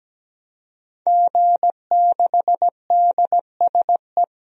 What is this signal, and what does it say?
Morse code, heard as a keyed tone, about 17 wpm, 700 Hz. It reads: G6DSE
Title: DXCC-100 Call Signs - 5 Characters - 17wpm